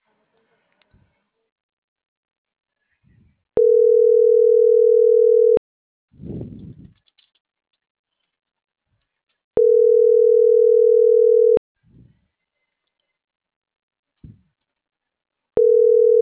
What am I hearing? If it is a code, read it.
Ringback tone (North America), 440 + 480 Hz.